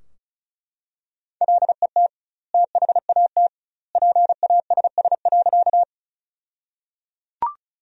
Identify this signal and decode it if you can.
Morse code, heard as a keyed tone, about 34 wpm, 700 Hz. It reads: LET THAT PASS.
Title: 2x Word-Spacing - Sentences from Top 400 Words - 35wpm